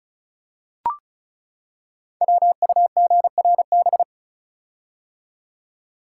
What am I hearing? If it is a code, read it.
Morse code, heard as a keyed tone, about 36 wpm, 700 Hz. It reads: WUGRB
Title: Code Groups - 5 Letters - 35wpm